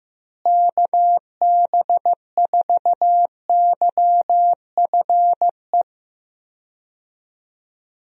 Morse code, heard as a keyed tone, about 15 wpm, 700 Hz. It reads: KB4YFE
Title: USA Call Signs - 6 Characters - Repeated 2x - 15wpm